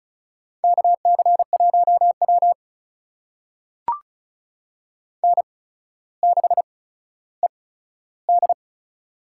Morse code, heard as a keyed tone, about 36 wpm, spaced as for 15 wpm, 700 Hz. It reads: KC1W N6ED
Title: USA Call Signs - 4 Characters - 8x Character Spacing - 35wpm